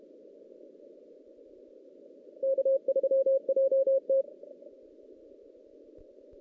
Morse code, heard as a keyed tone, about 32 wpm, 540 Hz. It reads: K3JT
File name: 2016 JAN CW NAQP / 2016 Jan naqp k5ot - 0-5